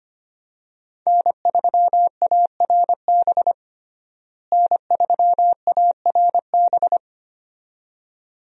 Morse code, heard as a keyed tone, about 25 wpm, 700 Hz. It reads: N3ARB N3ARB
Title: USA Call Signs - 5 Characters - Repeated 2x - 25wpm